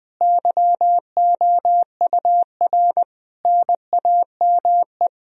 Morse code, heard as a keyed tone, about 20 wpm, 700 Hz. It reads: YOUR NAME
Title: Sentences from Top 1000 Words - Review Part 2 - 20wpm